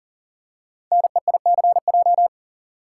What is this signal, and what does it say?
Morse code, heard as a keyed tone, about 41 wpm, 690 Hz. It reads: NEICJ
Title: Code Groups - 5 Letters - 40wpm